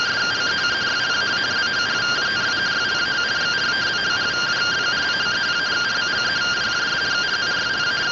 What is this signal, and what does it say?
Two-tone teleprinter signal, about 170 Hz shift.